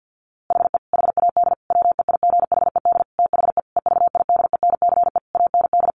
Morse code, heard as a keyed tone, about 50 wpm, 700 Hz. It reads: HE HAD WEIGHED THE EVIDENCE AND
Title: A Sound Of Thunder 50wpm